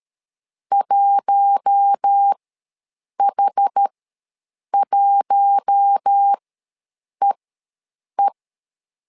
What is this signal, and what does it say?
Morse code, heard as a keyed tone, about 13 wpm, spaced as for 7 wpm, 800 Hz. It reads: 1H1EE
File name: LEZIONI / 4 Q1 ws 12wpm 2129